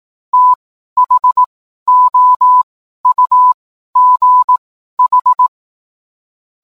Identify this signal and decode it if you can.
Morse code, heard as a keyed tone, about 18 wpm, spaced as for 12 wpm, 1000 Hz. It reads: THOUGH